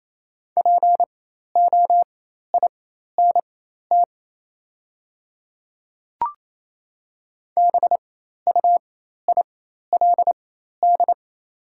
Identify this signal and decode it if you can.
Morse code, heard as a keyed tone, about 28 wpm, spaced as for 13 wpm, 700 Hz. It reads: POINT BUILD
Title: 4x Character Spacing - Top 200 Words - 28wpm